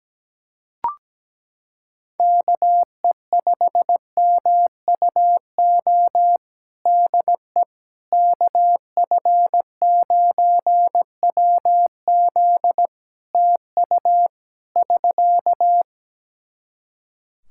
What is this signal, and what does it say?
Morse code, heard as a keyed tone, about 17 wpm, 700 Hz. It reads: KE5MUO DE KF9WZ TU <SK>
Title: K1USN SST QSO Practice 17wpm